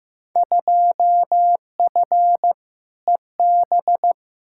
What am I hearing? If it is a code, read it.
Morse code, heard as a keyed tone, about 15 wpm, 700 Hz. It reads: 2F EB
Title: Field Day Exchanges Encore - 15wpm